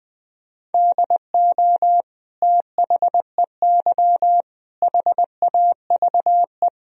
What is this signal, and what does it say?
Morse code, heard as a keyed tone, about 21 wpm, 700 Hz. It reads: DO THEY HAVE
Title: Sentences from Top 1000 Words - Review Part 1 - 20wpm